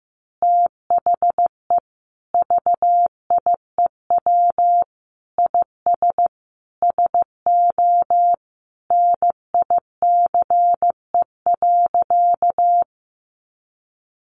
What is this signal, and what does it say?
Morse code, heard as a keyed tone, about 15 wpm, 700 Hz. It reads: THE VIEW IS SO NICE.
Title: Additional Easy Phrases 15wpm